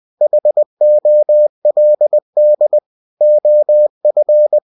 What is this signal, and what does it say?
Morse code, heard as a keyed tone, about 20 wpm, 600 Hz. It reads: HOLD OF
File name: cw-books / winnie-the-pooh / winnie-20wpm-page-0019